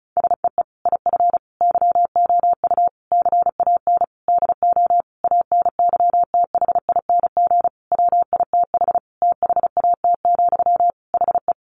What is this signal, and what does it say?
Morse code, heard as a keyed tone, about 34 wpm, 700 Hz. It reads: SEE IF YOU CAN DO ANYTHING WITH THAT, HE
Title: Sentences from Top 1000 Words 35wpm